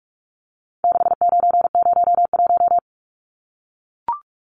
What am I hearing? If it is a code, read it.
Morse code, heard as a keyed tone, about 44 wpm, 700 Hz. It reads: 6901
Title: Numbers - 4-Digit 45wpm